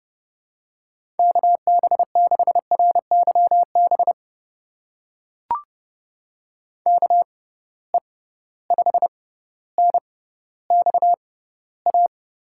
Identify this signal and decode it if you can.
Morse code, heard as a keyed tone, about 30 wpm, spaced as for 15 wpm, 700 Hz. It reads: KB6RYB KE5NXA
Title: USA Call Signs - 6 Characters - 6x Character Spacing - 30wpm